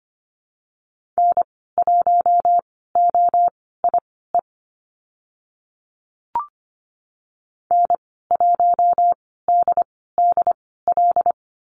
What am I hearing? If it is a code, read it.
Morse code, heard as a keyed tone, about 25 wpm, spaced as for 16 wpm, 700 Hz. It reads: N1OIE N1DDL
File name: Lesson 15g - Number 5 - Call Signs 25wpm